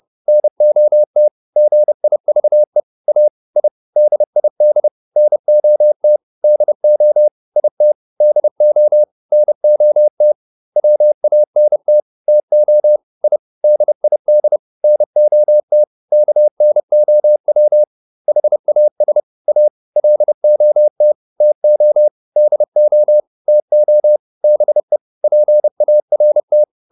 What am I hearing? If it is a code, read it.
Morse code, heard as a keyed tone, about 30 wpm, 600 Hz. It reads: NOT GIVE A I DID NOT DO IT DO NOT WANT TO I DID NOT KNOW HAS A LOT TO DO TO BE PART